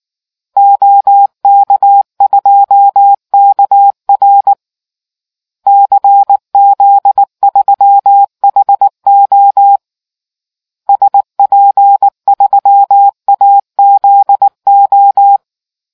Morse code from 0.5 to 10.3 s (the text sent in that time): OK2KR CZ3HO